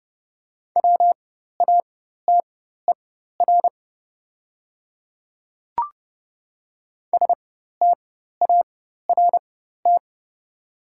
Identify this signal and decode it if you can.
Morse code, heard as a keyed tone, about 30 wpm, spaced as for 14 wpm, 690 Hz. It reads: WATER START
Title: 4x Character Spacing - Top 500 Words - 30wpm